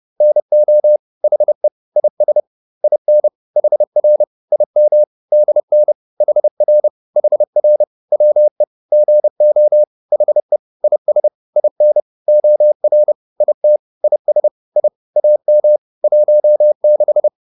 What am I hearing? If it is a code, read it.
Morse code, heard as a keyed tone, about 30 wpm, 600 Hz. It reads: NO HE IS IN HR IM DN HR HR WE GO HE IS IN OR IT IS I AM 16
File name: PR102 30